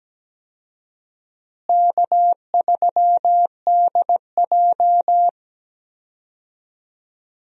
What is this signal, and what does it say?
Morse code, heard as a keyed tone, about 17 wpm, 700 Hz. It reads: K3DJ
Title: USA Call Signs - 4 Characters - Repeated 2x - 17wpm